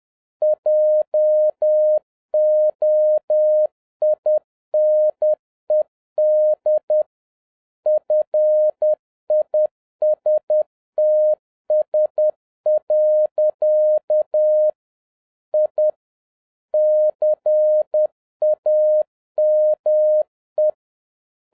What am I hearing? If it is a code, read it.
Morse code, heard as a keyed tone, about 10 wpm, 610 Hz. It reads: JOINED FISTS. I CAME